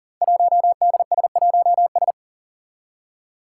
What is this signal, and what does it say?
Morse code, heard as a keyed tone, about 40 wpm, 700 Hz. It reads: 1DS1S
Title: Five Challenging Letter-Number Combinations 40wpm